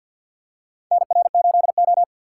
Morse code, heard as a keyed tone, about 53 wpm, 700 Hz. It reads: NR8O
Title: USA Call Signs - 4 Characters - 2x Character Spacing - 50wpm